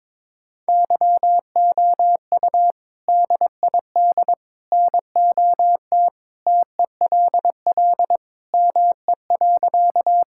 Morse code, heard as a keyed tone, about 22 wpm, 700 Hz. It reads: YOU DID NOT TELL ME.
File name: practice / 22wpm / Sentences with 3-Word Phrases - 22wpm